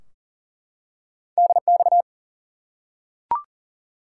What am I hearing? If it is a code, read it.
Morse code, heard as a keyed tone, about 40 wpm, 700 Hz. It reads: DX
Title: Top 90 QSO Elements 40wpm